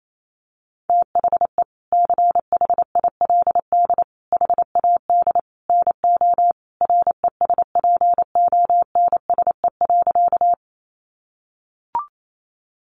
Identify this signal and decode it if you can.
Morse code, heard as a keyed tone, about 28 wpm, 700 Hz. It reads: THE CHILD HAD NO RESPONSE.
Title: Sentences from Top 1400 Words 28wpm